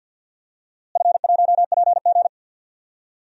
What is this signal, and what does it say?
Morse code, heard as a keyed tone, about 46 wpm, 700 Hz. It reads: F1PG
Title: DXCC-100 Call Signs - 4 Characters - Repeated 2x - 50wpm